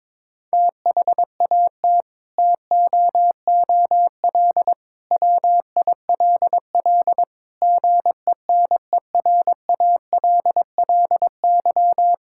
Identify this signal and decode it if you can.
Morse code, heard as a keyed tone, about 22 wpm, 700 Hz. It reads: THAT TOOL WILL GENERALLY
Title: Sentences from Top 1000 Words - Review Part 6 - 22wpm